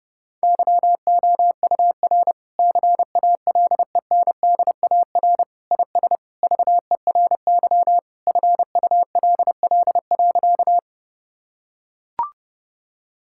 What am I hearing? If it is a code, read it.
Morse code, heard as a keyed tone, about 30 wpm, 700 Hz. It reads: YOUR CALENDAR IS VERY FULL.
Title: Sentences from Top 1600 Words 30wpm